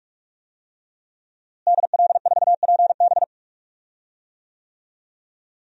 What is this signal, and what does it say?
Morse code, heard as a keyed tone, about 46 wpm, 700 Hz. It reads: DL4PB
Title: DXCC-100 Call Signs - 5 Characters - Repeated 2x - 45wpm